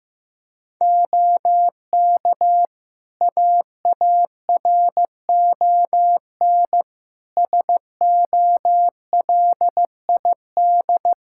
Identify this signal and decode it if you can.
Morse code, heard as a keyed tone, about 15 wpm, 700 Hz. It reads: OK AARON SOLID
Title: K1USN SST QSO Practice 15wpm